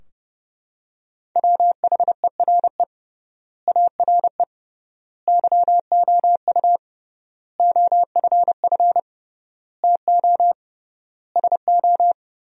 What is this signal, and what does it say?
Morse code, heard as a keyed tone, about 30 wpm, 700 Hz. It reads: WHERE ARE YOU OFF TO SO
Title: 3x Word-Spacing - Sentences from Top 400 Words - 30wpm